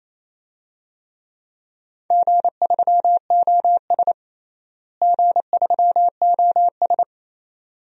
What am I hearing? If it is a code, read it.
Morse code, heard as a keyed tone, about 28 wpm, 700 Hz. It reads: G3OS G3OS
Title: DXCC-100 Call Signs - 4 Characters - Repeated 2x - 28wpm